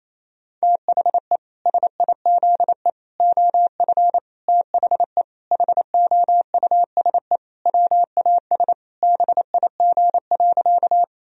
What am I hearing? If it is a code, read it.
Morse code, heard as a keyed tone, about 27 wpm, 700 Hz. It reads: THE SIZE OF THE HOUSE WAS BIG.
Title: Sentences from Top 1000 Words - Review Part 2 - 28wpm